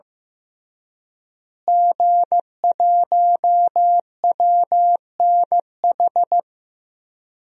Morse code, heard as a keyed tone, about 15 wpm, 700 Hz. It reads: G1WNH
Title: DXCC-100 Call Signs - 5 Characters - Repeated 3x - 15wpm